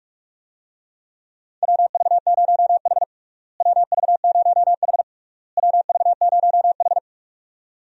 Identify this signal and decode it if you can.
Morse code, heard as a keyed tone, about 41 wpm, 700 Hz. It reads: WV0H WV0H WV0H
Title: USA Call Signs - 4 Characters - Repeated 3x - 45wpm